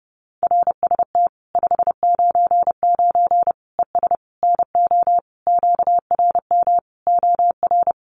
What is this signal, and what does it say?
Morse code, heard as a keyed tone, about 30 wpm, 700 Hz. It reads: RST 599 ES NO QRM OR
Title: Common QSO Phrases 30wpm